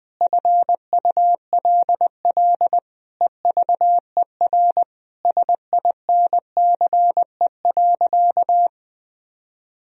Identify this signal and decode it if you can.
Morse code, heard as a keyed tone, about 20 wpm, 700 Hz. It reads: FULL EVER SINCE.
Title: Sentences from Top 1000 Words - Review Part 3 - 20wpm